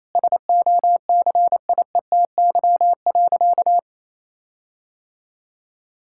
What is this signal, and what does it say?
Morse code, heard as a keyed tone, about 28 wpm, 700 Hz. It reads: SOCIETY.
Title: Sentences from Top 1600 Words 28wpm